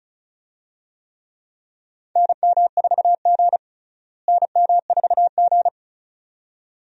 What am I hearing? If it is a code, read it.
Morse code, heard as a keyed tone, about 36 wpm, 700 Hz. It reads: NM4G NM4G
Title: USA Call Signs - 4 Characters - Repeated 2x - 35wpm